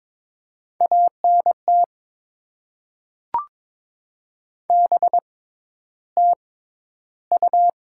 Morse code, heard as a keyed tone, about 22 wpm, spaced as for 11 wpm, 700 Hz. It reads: ANT BTU
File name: Top 20 QSO Elements - 6x Character Spacing - 22wpm